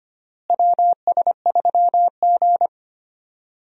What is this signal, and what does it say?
Morse code, heard as a keyed tone, about 25 wpm, 700 Hz. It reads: WS3G
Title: USA Call Signs - 4 Characters - Repeated 3x - 25wpm